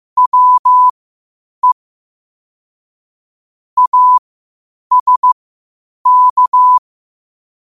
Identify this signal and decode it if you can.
Morse code, heard as a keyed tone, about 15 wpm, spaced as for 8 wpm, 1000 Hz. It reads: WE ASK
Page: New quotes every day in morse code at 8 Words per minute.